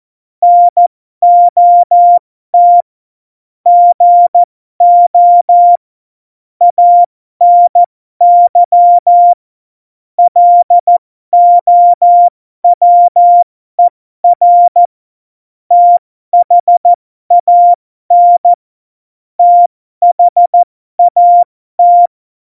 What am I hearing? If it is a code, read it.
Morse code, heard as a keyed tone, about 14 wpm, spaced as for 12 wpm, 700 Hz. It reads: NOT GO ANY LOWER THAN THAT
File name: Morsekurs / Witze